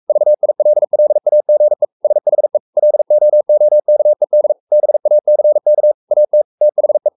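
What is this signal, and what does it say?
Morse code, heard as a keyed tone, about 44 wpm, 600 Hz. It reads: VILLAGE SHE LOOKED BACK AT THE
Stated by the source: Morse code at 43 wpm sending Ken Follett's novel Never in English, from CW med Gnister.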